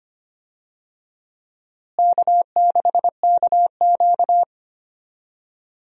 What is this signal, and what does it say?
Morse code, heard as a keyed tone, about 25 wpm, 700 Hz. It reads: K6KQ
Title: USA Call Signs - 4 Characters - Repeated 2x - 25wpm